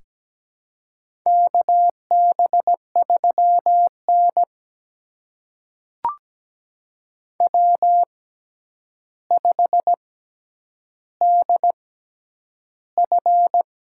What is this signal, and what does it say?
Morse code, heard as a keyed tone, about 17 wpm, spaced as for 9 wpm, 700 Hz. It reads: KB3N W5DF
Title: USA Call Signs - 4 Characters - 6x Character Spacing - 17wpm